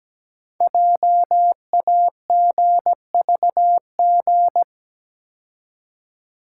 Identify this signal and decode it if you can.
Morse code, heard as a keyed tone, about 17 wpm, 700 Hz. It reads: JAGVG
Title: Code Groups - 5 Letters - 17wpm